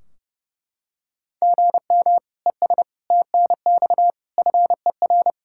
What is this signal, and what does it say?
Morse code, heard as a keyed tone, about 32 wpm, 700 Hz. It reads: GM ES TNX FER
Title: Practice QSO - Exchange 1 - Beginning - 30wpm